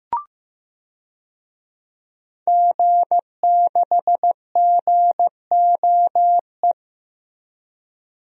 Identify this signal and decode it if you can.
Morse code, heard as a keyed tone, about 15 wpm, 700 Hz. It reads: G6GOE
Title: DXCC-100 Call Signs - 5 Characters - Repeated 2x - 15wpm